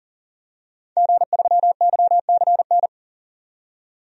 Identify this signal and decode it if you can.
Morse code, heard as a keyed tone, about 40 wpm, 700 Hz. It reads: G3YCN
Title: DXCC-100 Call Signs - 5 Characters - 40wpm